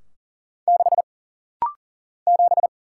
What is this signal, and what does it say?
Morse code, heard as a keyed tone, about 40 wpm, 700 Hz. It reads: 6 7
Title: Single Letter-Number - Rapid-Fire 40wpm